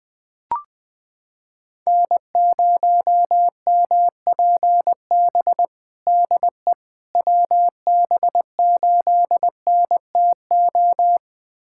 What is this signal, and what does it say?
Morse code, heard as a keyed tone, about 20 wpm, 700 Hz. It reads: N0MPB DE WB8NTO
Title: USA Callsign DE Callsign 20wpm